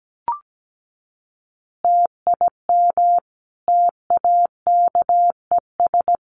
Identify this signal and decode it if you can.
Morse code, heard as a keyed tone, about 17 wpm, 700 Hz. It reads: TIM TAKES
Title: Easy Phrases 17wpm